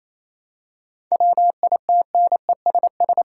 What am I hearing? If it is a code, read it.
Morse code, heard as a keyed tone, about 28 wpm, 700 Hz. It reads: WITNESS
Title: Seven Letter Words - 28wpm